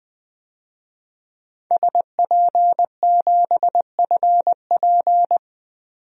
Morse code, heard as a keyed tone, about 20 wpm, 700 Hz. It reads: SP7FP
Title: DXCC-100 Call Signs - 5 Characters - 20wpm